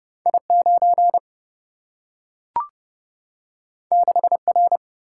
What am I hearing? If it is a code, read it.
Morse code, heard as a keyed tone, about 30 wpm, 700 Hz. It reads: I 9 6 R
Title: Single-Letter Single-Number 30wpm